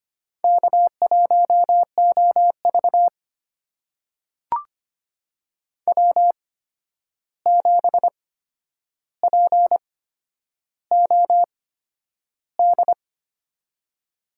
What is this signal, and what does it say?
Morse code, heard as a keyed tone, about 25 wpm, spaced as for 10 wpm, 700 Hz. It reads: K1OV W7POD
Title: USA Call Signs - 8x Character Spacing - 25wpm